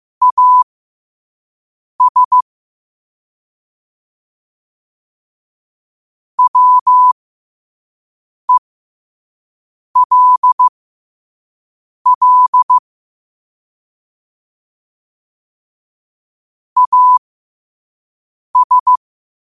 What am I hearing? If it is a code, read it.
Morse code, heard as a keyed tone, about 15 wpm, spaced as for 5 wpm, 1000 Hz. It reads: AS WELL AS
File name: qotdfiles / 5wpm